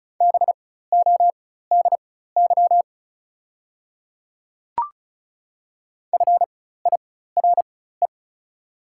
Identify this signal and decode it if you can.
Morse code, heard as a keyed tone, about 34 wpm, spaced as for 16 wpm, 690 Hz. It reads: BODY FIRE
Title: 4x Character Spacing - Top 500 Words - 35wpm